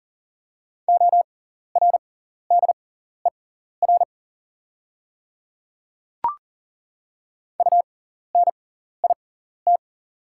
Morse code, heard as a keyed tone, about 39 wpm, spaced as for 14 wpm, 690 Hz. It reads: ORDER UNIT
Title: 6x Character Spacing - Top 500 Words - 40wpm